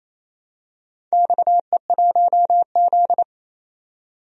Morse code, heard as a keyed tone, about 29 wpm, 700 Hz. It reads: XE1Z
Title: DXCC-100 Call Signs - 4 Characters - 28wpm